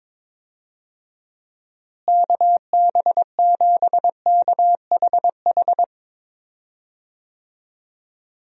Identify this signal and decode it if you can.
Morse code, heard as a keyed tone, about 22 wpm, 700 Hz. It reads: KB7KHH
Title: USA Call Signs - 6 Characters - Repeated 2x - 22wpm